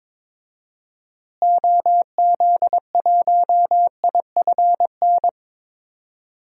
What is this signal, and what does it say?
Morse code, heard as a keyed tone, about 22 wpm, 700 Hz. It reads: OZ1IFN
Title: DXCC-100 Call Signs - 6 Characters - Repeated 3x - 22wpm